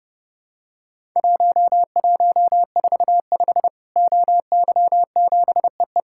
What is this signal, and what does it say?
Morse code, heard as a keyed tone, about 30 wpm, 700 Hz. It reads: 1145 OY7EE
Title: Worked All Europe DX Contest - QTC Exchange - 30wpm